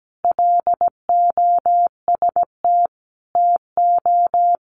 Morse code, heard as a keyed tone, about 17 wpm, 700 Hz. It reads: LOST TO
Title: Sentences from Top 1400 Words 17wpm